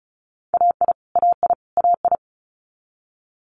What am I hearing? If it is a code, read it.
Morse code, heard as a keyed tone, about 36 wpm, 700 Hz. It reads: AI AI AI
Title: Multi-letter Phonemes Repeated 3x 35wpm